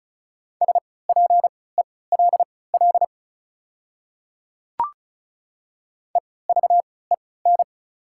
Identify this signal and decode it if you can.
Morse code, heard as a keyed tone, about 34 wpm, spaced as for 20 wpm, 690 Hz. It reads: SPELL EVEN
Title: 3x Character Spacing - Top 200 Words - 35wpm